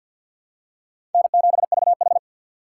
Morse code, heard as a keyed tone, about 52 wpm, 700 Hz. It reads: N7VH
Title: USA Call Signs - 4 Characters - 2x Character Spacing - 50wpm